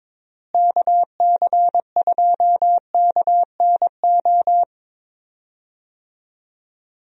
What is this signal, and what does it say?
Morse code, heard as a keyed tone, about 22 wpm, 700 Hz. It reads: KC2KNO
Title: USA Call Signs - 6 Characters - Repeated 2x - 22wpm